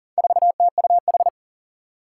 Morse code, heard as a keyed tone, about 40 wpm, 690 Hz. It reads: 4TUH
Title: Four Challenging Letter-Number Combinations 40wpm